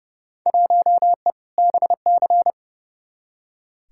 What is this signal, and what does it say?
Morse code, heard as a keyed tone, about 30 wpm, 700 Hz. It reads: 1E BC
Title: Field Day Exchanges - 30wpm